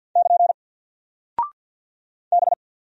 Morse code, heard as a keyed tone, about 50 wpm, 700 Hz. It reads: <KN> B
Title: Single Character (Letter:Number:Punctuation:Prosign) Alt 50wpm